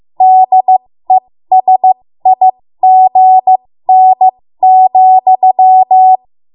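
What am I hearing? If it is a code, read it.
Morse code, heard as a keyed tone, about 15 wpm, spaced as for 13 wpm, 750 Hz. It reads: DESIGN,
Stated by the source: ARRL Morse code practice at 13 WPM.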